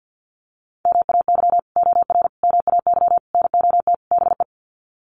Morse code, heard as a keyed tone, about 50 wpm, 700 Hz. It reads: MAY OR MAY NOT BE
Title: Sets of 5 Words 50wpm